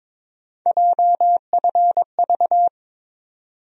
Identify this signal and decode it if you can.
Morse code, heard as a keyed tone, about 22 wpm, 700 Hz. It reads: JFV
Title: USA Call Sign Postfixes - 3 Letters 22wpm